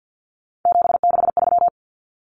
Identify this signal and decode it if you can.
Morse code, heard as a keyed tone, about 50 wpm, 700 Hz. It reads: 763
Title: Numbers - 3-Digit 50wpm